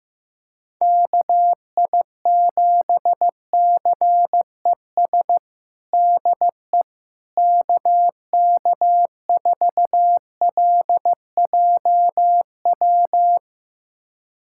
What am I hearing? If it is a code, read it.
Morse code, heard as a keyed tone, about 15 wpm, 700 Hz. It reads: KI7CES DE KK4LJW
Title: USA Callsign DE Callsign 15wpm